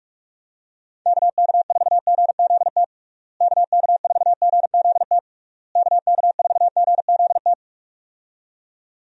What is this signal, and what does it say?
Morse code, heard as a keyed tone, about 46 wpm, 700 Hz. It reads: KK4GZT KK4GZT KK4GZT
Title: USA Call Signs - 6 Characters - Repeated 3x - 45wpm